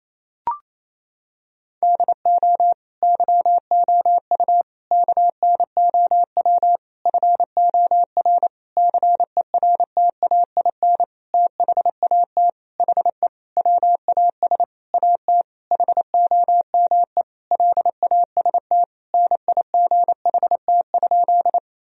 Morse code, heard as a keyed tone, about 28 wpm, 700 Hz. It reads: DO YOU KNOW FOR CERTAIN THAT HE WAS AT HOME LAST NIGHT?
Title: Sentences from Top 1000 Words - Review Part 2 - 28wpm